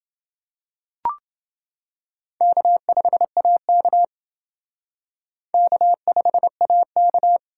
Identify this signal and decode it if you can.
Morse code, heard as a keyed tone, about 28 wpm, 700 Hz. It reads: K5AK K5AK